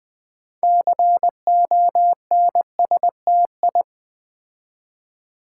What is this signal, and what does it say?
Morse code, heard as a keyed tone, about 20 wpm, 700 Hz. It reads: CONSTI
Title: Build A Word - 6 Letters At A Time - 12-Letter Words - 20wpm